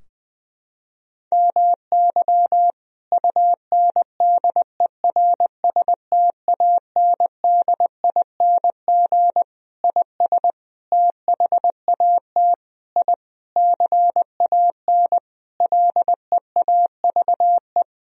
Morse code, heard as a keyed tone, about 20 wpm, 700 Hz. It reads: MY UNDERSTANDING IS THAT I CAN LEAVE
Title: Sentences from Top 1000 Words - Review Part 3 - 20wpm